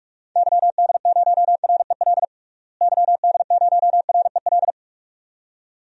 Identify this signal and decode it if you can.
Morse code, heard as a keyed tone, about 46 wpm, 700 Hz. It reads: YD0REL YD0REL
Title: DXCC-100 Call Signs - 6 Characters - Repeated 2x - 45wpm